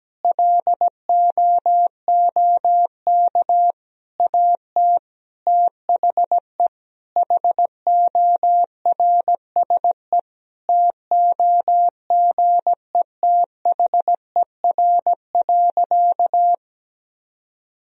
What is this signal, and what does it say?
Morse code, heard as a keyed tone, about 17 wpm, 700 Hz. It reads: LOOK AT THE HORSE TOGETHER.
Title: Sentences from Top 1000 Words - Review Part 1 - 17wpm